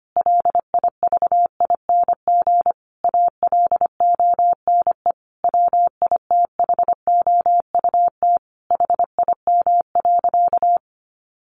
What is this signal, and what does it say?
Morse code, heard as a keyed tone, about 25 wpm, 700 Hz. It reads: LIVING ALONE WITHOUT HIM.
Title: Sentences from Top 1400 Words 25wpm